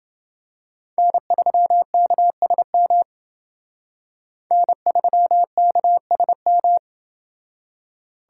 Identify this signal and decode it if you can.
Morse code, heard as a keyed tone, about 28 wpm, 700 Hz. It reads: N3KSM N3KSM